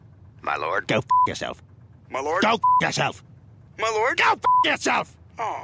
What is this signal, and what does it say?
Speech. There is no code.